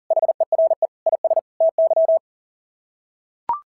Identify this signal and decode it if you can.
Morse code, heard as a keyed tone, about 37 wpm, 660 Hz. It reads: HERE IS TY
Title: SOTA QSO Practice 40wpm